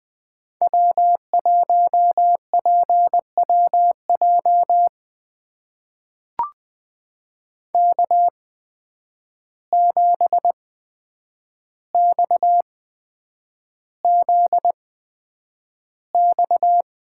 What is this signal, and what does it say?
Morse code, heard as a keyed tone, about 20 wpm, spaced as for 9 wpm, 700 Hz. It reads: W1PWJ K7XZX
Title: USA Call Signs - 5 Characters - 8x Character Spacing - 20wpm